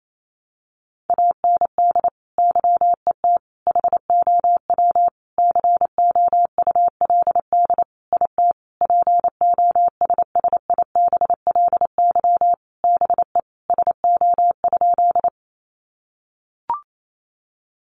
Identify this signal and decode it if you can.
Morse code, heard as a keyed tone, about 28 wpm, 700 Hz. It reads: AND YET HOW COULD IT POSSIBLY BE SO?
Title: Sentences from Top 1300 Words 28wpm